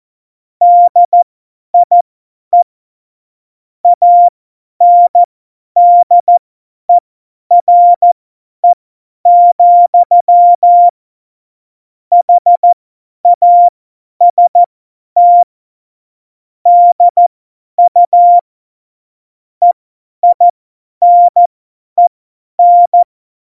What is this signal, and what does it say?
Morse code, heard as a keyed tone, about 14 wpm, spaced as for 10 wpm, 700 Hz. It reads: DIE ANDERE, HAST DU EINEN
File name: Morsekurs / Witze / Witz 25. September 2018, 50 F70